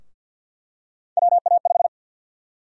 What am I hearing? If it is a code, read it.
Morse code, heard as a keyed tone, about 46 wpm, 700 Hz. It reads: WA5
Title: USA Call Sign Prefixes - Repeated 2x - 50wpm